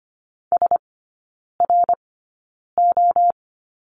Morse code, heard as a keyed tone, about 25 wpm, spaced as for 9 wpm, 700 Hz. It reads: SRO
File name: Lesson 15e - Number 5 - Three Characters 25wpm